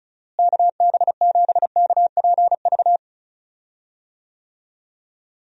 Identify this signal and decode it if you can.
Morse code, heard as a keyed tone, about 34 wpm, 700 Hz. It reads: KB7KPV
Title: USA Call Signs - 6 Characters - Repeated 3x - 35wpm